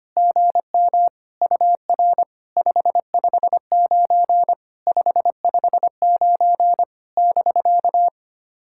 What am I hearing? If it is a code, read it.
Morse code, heard as a keyed tone, about 25 wpm, 700 Hz. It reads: GM UR 559 559 <BK>
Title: SOTA QSO Practice 25wpm